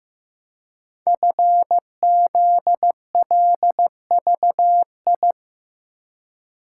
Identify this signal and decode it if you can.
Morse code, heard as a keyed tone, about 15 wpm, 700 Hz. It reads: FZLVI
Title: Code Groups - 5 Letters - 15wpm